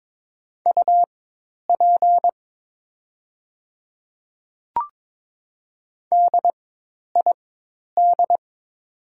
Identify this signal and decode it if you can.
Morse code, heard as a keyed tone, about 22 wpm, spaced as for 10 wpm, 700 Hz. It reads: UP DID
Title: 4x Character Spacing - Top 100 Words - 22wpm